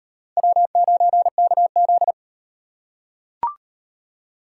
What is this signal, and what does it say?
Morse code, heard as a keyed tone, about 36 wpm, 700 Hz. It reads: W9KZ